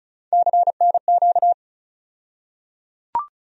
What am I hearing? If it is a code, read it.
Morse code, heard as a keyed tone, about 36 wpm, 700 Hz. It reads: CNQ
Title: USA Call Sign Postfixes - 3 Letters 35wpm